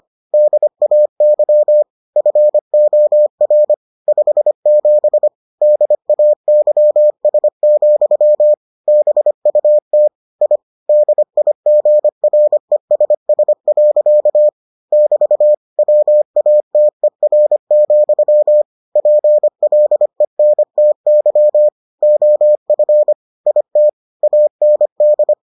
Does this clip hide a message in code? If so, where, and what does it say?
Yes: Morse code from 0.3 to 25.3 s, heard as a keyed tone, about 25 wpm, 600 Hz. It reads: DAY FOR 57 DAYS, BUT I DIGRESS. <BT> WATER, PLENTY OF IT AND